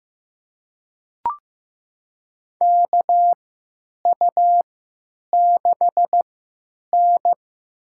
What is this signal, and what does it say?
Morse code, heard as a keyed tone, about 15 wpm, spaced as for 9 wpm, 700 Hz. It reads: KU6N
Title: USA Call Signs - 4 Characters - 3x Character Spacing - 15wpm